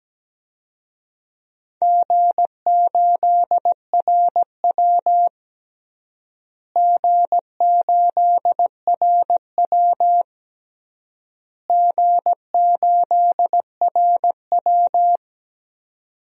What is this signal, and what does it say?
Morse code, heard as a keyed tone, about 17 wpm, 700 Hz. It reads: G8RW G8RW G8RW
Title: DXCC-100 Call Signs - 4 Characters - Repeated 3x - 17wpm